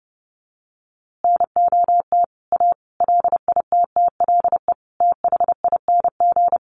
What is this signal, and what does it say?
Morse code, heard as a keyed tone, about 30 wpm, 700 Hz. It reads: NOT A LITTLE THING
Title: A Sound Of Thunder 30wpm